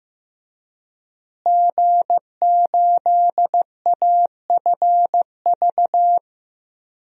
Morse code, heard as a keyed tone, about 15 wpm, 700 Hz. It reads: G8AFV
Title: DXCC-100 Call Signs - 5 Characters - Repeated 2x - 15wpm